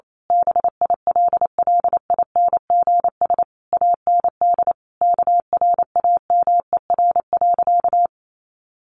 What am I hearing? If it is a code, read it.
Morse code, heard as a keyed tone, about 28 wpm, 700 Hz. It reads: BILLINGS AND KRAMER.
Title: A Sound Of Thunder 28wpm